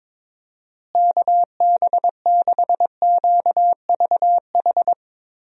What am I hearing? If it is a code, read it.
Morse code, heard as a keyed tone, about 22 wpm, 700 Hz. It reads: KB6QVH
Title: USA Call Signs - 6 Characters - Repeated 3x - 22wpm